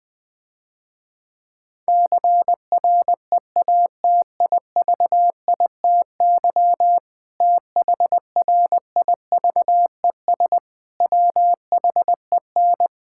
Morse code, heard as a keyed tone, about 21 wpm, 700 Hz. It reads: CREATIVITY THRIVES WHEN
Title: Difficult Sentences with Hints - 20wpm